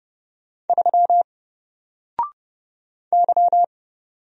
Morse code, heard as a keyed tone, about 30 wpm, 700 Hz. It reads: 3 Y
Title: Single Letter-Number Alt 30wpm